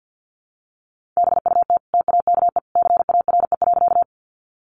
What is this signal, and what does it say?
Morse code, heard as a keyed tone, about 50 wpm, 700 Hz. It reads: BUT TAKE CARE.
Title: Sentences from Top 300 Words 50wpm